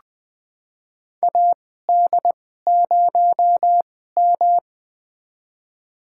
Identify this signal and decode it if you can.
Morse code, heard as a keyed tone, about 20 wpm, spaced as for 15 wpm, 700 Hz. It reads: AD0M
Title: USA Call Signs - 2x Character Spacing - 20wpm